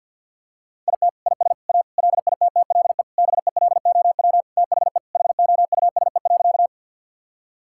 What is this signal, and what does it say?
Morse code, heard as a keyed tone, about 50 wpm, 700 Hz. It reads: IT IS A LITTLE BELOW THE HOUSE.
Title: Sentences from Top 1000 Words - Review Part 5 - 50wpm